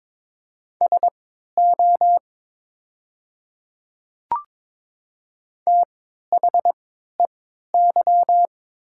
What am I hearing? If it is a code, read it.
Morse code, heard as a keyed tone, about 22 wpm, spaced as for 12 wpm, 700 Hz. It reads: SO THEY
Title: 3x Character Spacing - Top 100 Words - 22wpm